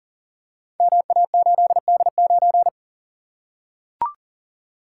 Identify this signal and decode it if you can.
Morse code, heard as a keyed tone, about 42 wpm, 700 Hz. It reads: MA8D9
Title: Code Groups - 5 Letter:Numbers - 40wpm